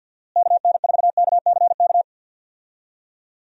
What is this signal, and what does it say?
Morse code, heard as a keyed tone, about 50 wpm, 700 Hz. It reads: KN4KCK
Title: USA Call Signs - 6 Characters - Repeated 2x - 50wpm